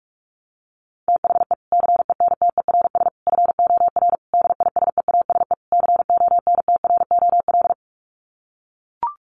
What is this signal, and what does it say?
Morse code, heard as a keyed tone, about 44 wpm, 700 Hz. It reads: THE CENTERS FOR DISEASE CONTROL
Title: Sets of 5 Words Encore2 45wpm